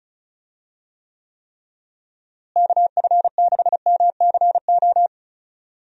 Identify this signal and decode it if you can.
Morse code, heard as a keyed tone, about 34 wpm, 700 Hz. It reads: KF6MCO
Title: USA Call Signs - 6 Characters - Repeated 3x - 35wpm